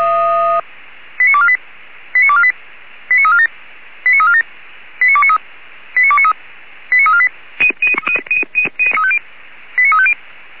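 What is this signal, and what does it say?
Five-tone selective call (ZVEI), heard as quick runs of five tones, tones of about 70 ms. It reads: 89248 89248 89347 89347 89293 89293 89348 89350
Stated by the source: ZVEI five-tone call 89461 received by SDR on channel 464.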